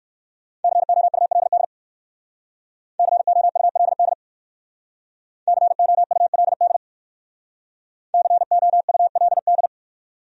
Morse code, heard as a keyed tone, about 59 wpm, 690 Hz. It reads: COULD COULD COULD COULD